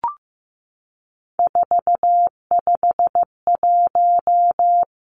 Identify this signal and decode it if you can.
Morse code, heard as a keyed tone, about 15 wpm, 700 Hz. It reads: 451
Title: Three Challenging Letter-Number Combinations 15wpm